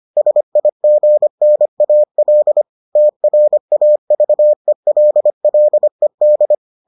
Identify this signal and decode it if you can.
Morse code, heard as a keyed tone, about 25 wpm, 600 Hz. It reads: SIGNAL TRAVELLED